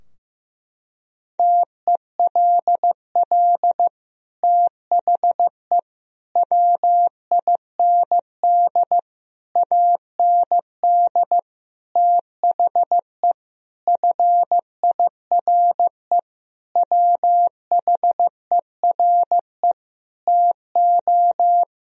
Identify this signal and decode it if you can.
Morse code, heard as a keyed tone, about 15 wpm, 700 Hz. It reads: TELL THE WIND AND THE FIRE WHERE TO
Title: Sentences from Top 1000 Words - Review Part 2 - 15wpm